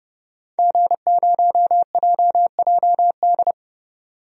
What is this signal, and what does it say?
Morse code, heard as a keyed tone, about 30 wpm, 700 Hz. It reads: G0JJD
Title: DXCC-100 Call Signs - 5 Characters - 30wpm